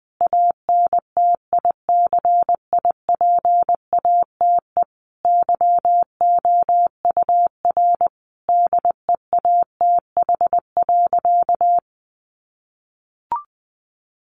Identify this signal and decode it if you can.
Morse code, heard as a keyed tone, about 20 wpm, 700 Hz. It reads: ANTICIPATE YOUR DEATH.
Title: Sentences from Top 1500 Words 20wpm